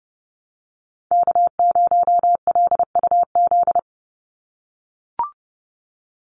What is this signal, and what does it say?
Morse code, heard as a keyed tone, about 30 wpm, 700 Hz. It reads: K0LUZ
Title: Call Signs - 30wpm